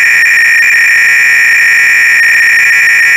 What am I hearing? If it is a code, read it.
300-baud modem data.
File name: Repository / ACR-Audio-Cassette-Programs / MITS-BASIC-GAME-BATTLE-CLOADC